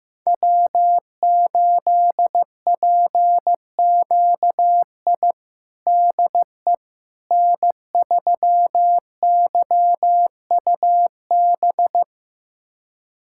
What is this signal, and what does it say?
Morse code, heard as a keyed tone, about 15 wpm, 700 Hz. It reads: W8PQI DE N3YUB
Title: USA Callsign DE Callsign - 5 Characters - 15wpm